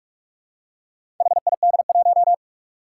Morse code, heard as a keyed tone, about 46 wpm, 700 Hz. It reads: HID1
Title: Four Challenging Letter-Number Combinations 45wpm